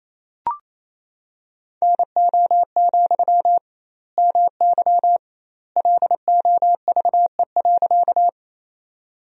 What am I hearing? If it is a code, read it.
Morse code, heard as a keyed tone, about 28 wpm, 700 Hz. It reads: NO, MY LOVE.
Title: 2x Word-Spacing - Sentences from Top 400 Words - 28wpm